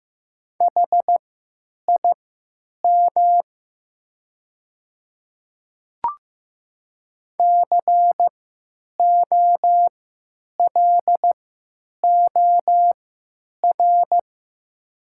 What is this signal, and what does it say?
Morse code, heard as a keyed tone, about 15 wpm, spaced as for 9 wpm, 700 Hz. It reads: HIM COLOR
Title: 3x Character Spacing - Top 300 Words - 15wpm